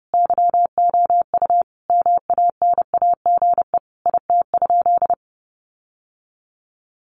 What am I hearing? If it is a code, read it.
Morse code, heard as a keyed tone, about 30 wpm, 700 Hz. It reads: YOU MANAGE IT?
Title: Sentences from Top 1400 Words 30wpm